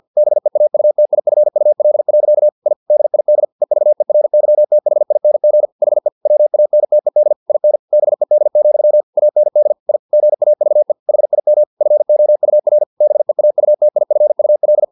Morse code, heard as a keyed tone, about 50 wpm, 600 Hz. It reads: BEAUTIFUL. I DID EVERYTHING HE WANTED IN BED, AND I GAVE HIM FOUR BEAUTIFUL
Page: Never 49wpm